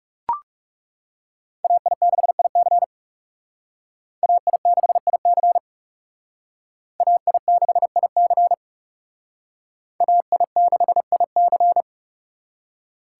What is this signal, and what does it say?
Morse code, heard as a keyed tone, about 37 wpm, 700 Hz. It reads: AI6IC AI6IC AI6IC AI6IC
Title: Speed Racing - USA 2x2 Call Signs 30wpm